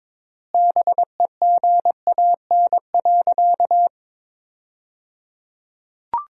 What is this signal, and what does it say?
Morse code, heard as a keyed tone, about 22 wpm, 700 Hz. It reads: BEGAN.
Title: 3x Word-Spacing - Sentences from Top 300 Words - 22wpm